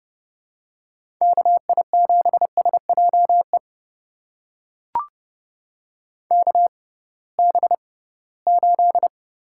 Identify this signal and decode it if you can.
Morse code, heard as a keyed tone, about 31 wpm, spaced as for 20 wpm, 700 Hz. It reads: KI7SJE KB8
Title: USA Call Signs - 6x Character Spacing - 30wpm